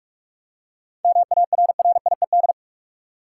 Morse code, heard as a keyed tone, about 46 wpm, 700 Hz. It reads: MARRIED
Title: Seven Letter Words - 45wpm